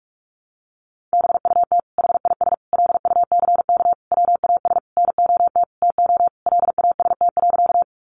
Morse code, heard as a keyed tone, about 44 wpm, 700 Hz. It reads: BUT HIS LUCK WAS NOT TO LAST.
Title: Sentences from Top 1500 Words 45wpm